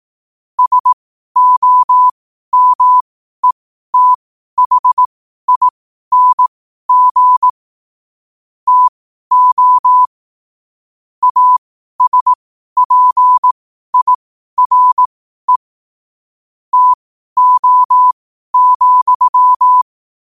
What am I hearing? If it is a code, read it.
Morse code, heard as a keyed tone, about 18 wpm, spaced as for 12 wpm, 1000 Hz. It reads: SOMETHING TO ASPIRE TO,
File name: qotdfiles / 12wpm